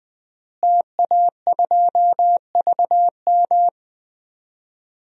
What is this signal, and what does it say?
Morse code, heard as a keyed tone, about 21 wpm, 700 Hz. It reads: TA2VM
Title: DXCC-100 Call Signs - 5 Characters - Repeated 3x - 20wpm